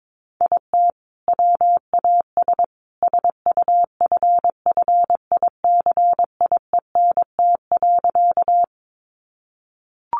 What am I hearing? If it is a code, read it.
Morse code, heard as a keyed tone, about 22 wpm, 700 Hz. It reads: IT WAS SUFFICIENT.
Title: Sentences from Top 1400 Words 22wpm